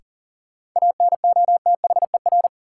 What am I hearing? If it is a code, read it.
Morse code, heard as a keyed tone, about 40 wpm, 700 Hz. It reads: ANOTHER
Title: Sentences from Top 600 Words - New Word Drill - 40wpm